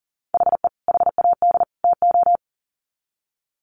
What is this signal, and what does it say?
Morse code, heard as a keyed tone, about 40 wpm, 700 Hz. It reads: HE HAD TO
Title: Sets of 3 Words 40wpm